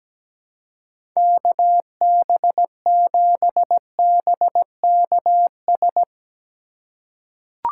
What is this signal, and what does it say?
Morse code, heard as a keyed tone, about 17 wpm, 700 Hz. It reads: KB7BKS
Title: USA Call Signs - 6 Characters - 3x Character Spacing - 17wpm